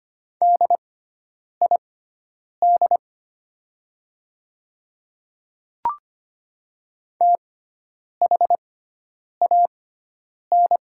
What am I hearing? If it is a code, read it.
Morse code, heard as a keyed tone, about 25 wpm, spaced as for 9 wpm, 690 Hz. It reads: DID THAN
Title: 6x Character Spacing - Top 200 Words - 25wpm